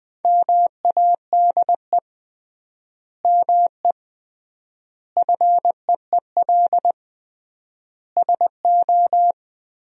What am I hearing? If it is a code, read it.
Morse code, heard as a keyed tone, about 20 wpm, 700 Hz. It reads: MADE ME FEEL SO
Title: 3x Word-Spacing - Sentences from Top 400 Words - 20wpm